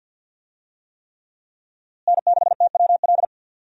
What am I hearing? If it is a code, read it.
Morse code, heard as a keyed tone, about 48 wpm, 700 Hz. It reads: N6TWL
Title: USA Call Signs - 5 Characters - Repeated 3x - 50wpm